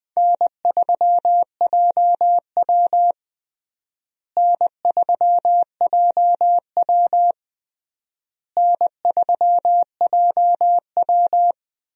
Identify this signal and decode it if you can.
Morse code, heard as a keyed tone, about 20 wpm, 700 Hz. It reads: N3JW N3JW N3JW
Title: USA Call Signs - 4 Characters - Repeated 3x - 20wpm